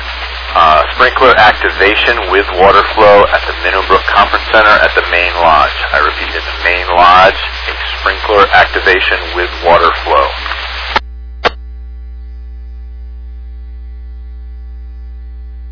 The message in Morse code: none

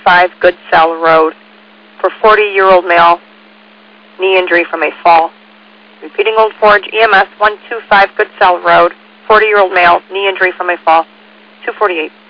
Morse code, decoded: none

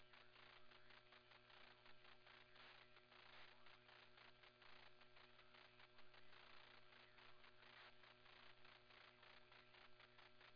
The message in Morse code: none

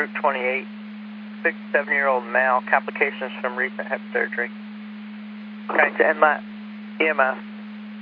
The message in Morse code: none

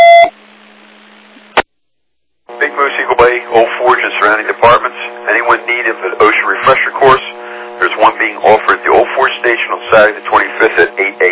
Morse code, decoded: none

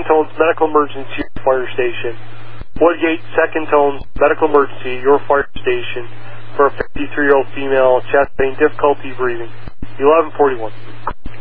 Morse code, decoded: none

